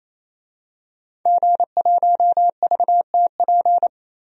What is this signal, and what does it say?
Morse code, heard as a keyed tone, about 28 wpm, 700 Hz. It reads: G1VTP
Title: DXCC-100 Call Signs - 5 Characters - 28wpm